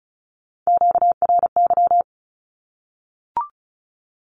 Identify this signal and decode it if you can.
Morse code, heard as a keyed tone, about 34 wpm, 700 Hz. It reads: QRY
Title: Q-Codes Repeated 3x 35wpm